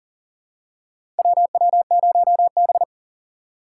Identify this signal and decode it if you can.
Morse code, heard as a keyed tone, about 40 wpm, 700 Hz. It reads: WW0B
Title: USA Call Signs - 6x Character Spacing - 40wpm